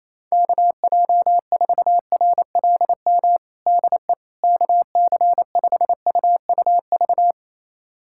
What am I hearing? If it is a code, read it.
Morse code, heard as a keyed tone, about 28 wpm, 700 Hz. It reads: KJ4RLM DE KC5UUV
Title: USA Callsign DE Callsign - 6 Characters - 28wpm